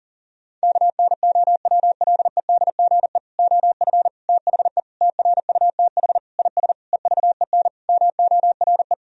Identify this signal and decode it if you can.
Morse code, heard as a keyed tone, about 40 wpm, 700 Hz. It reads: KNOWLEDGE OF THE TRUTH IS EVEN MORE
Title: Sentences from Top 1000 Words - Review Part 3 - 40wpm